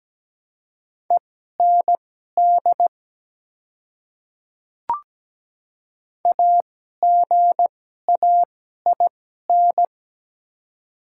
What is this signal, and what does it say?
Morse code, heard as a keyed tone, about 17 wpm, spaced as for 12 wpm, 700 Hz. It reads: END AGAIN
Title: 2x Character Spacing - Top 200 Words - 17wpm